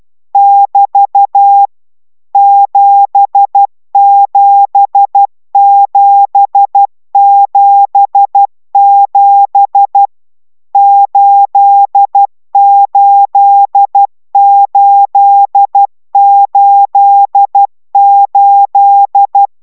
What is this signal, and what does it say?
Morse code, heard as a keyed tone, about 12 wpm, 800 Hz. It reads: <BT> 77777 88888